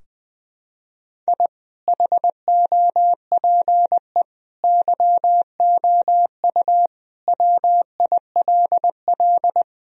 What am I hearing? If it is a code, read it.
Morse code, heard as a keyed tone, about 20 wpm, 700 Hz. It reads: I HOPE YOU WILL
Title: Sentences from Top 1000 Words - Review Part 4 - 20wpm